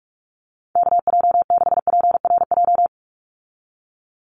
Morse code, heard as a keyed tone, about 44 wpm, 700 Hz. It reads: KJ6PRJ
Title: USA Call Signs 45wpm